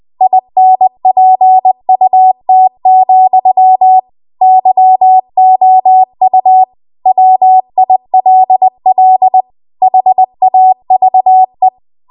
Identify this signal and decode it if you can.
Morse code, heard as a keyed tone, about 20 wpm, 750 Hz. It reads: INPUT, YOU WILL HAVE
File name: Archive / 20 WPM / 20WPM